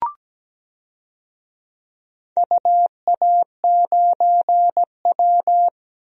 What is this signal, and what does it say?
Morse code, heard as a keyed tone, about 17 wpm, 700 Hz. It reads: UA9W
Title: DXCC-100 Call Signs - 4 Characters - Repeated 2x - 17wpm